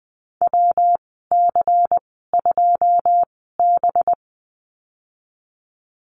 Morse code, heard as a keyed tone, about 20 wpm, spaced as for 14 wpm, 700 Hz. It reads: WC2B
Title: 193 Lesson 31g - Letter v - Call Signs 20wpm